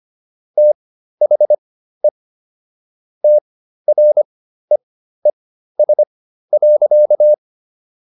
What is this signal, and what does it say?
Morse code, heard as a keyed tone, about 25 wpm, spaced as for 13 wpm, 600 Hz. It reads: THE TREES.